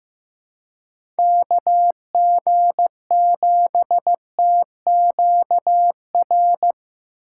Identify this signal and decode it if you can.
Morse code, heard as a keyed tone, about 15 wpm, 700 Hz. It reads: KG7TQR
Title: USA Call Signs - 6 Characters - 4x Character Spacing - 15wpm